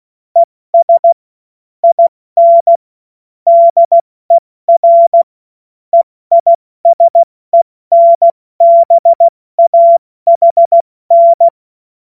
Morse code, heard as a keyed tone, about 16 wpm, spaced as for 14 wpm, 680 Hz. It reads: ES IN DER EISENBAHN
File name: Morsekurs / Witze / Witz 21. Juni 2022 70 F80